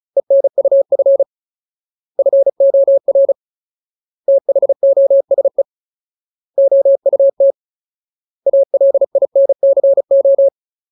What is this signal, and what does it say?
Morse code, heard as a keyed tone, about 34 wpm, 550 Hz. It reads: ENUF FOR THOSE OUT ALINCO
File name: Ultimate Word List / 35WPM Ultimate